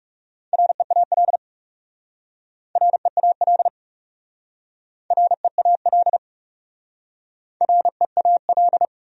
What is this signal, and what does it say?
Morse code, heard as a keyed tone, about 40 wpm, 700 Hz. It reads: REAL REAL REAL REAL